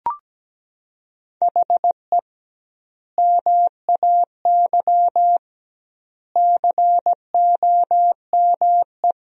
Morse code, heard as a keyed tone, about 17 wpm, 700 Hz. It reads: HE MAY COME
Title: 2x Word-Spacing - Sentences from Top 300 Words - 17wpm